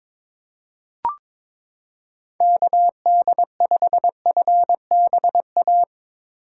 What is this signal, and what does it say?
Morse code, heard as a keyed tone, about 22 wpm, 700 Hz. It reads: KD5FBA
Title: K1USN SST QSO Practice - 22wpm